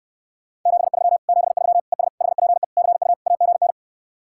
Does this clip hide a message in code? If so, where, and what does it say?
Yes: Morse code from 0.7 to 3.7 s, heard as a keyed tone, about 64 wpm, 700 Hz. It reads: 73 73 ES HPE CU AGN